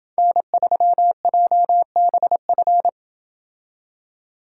Morse code, heard as a keyed tone, about 28 wpm, 700 Hz. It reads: N3JBF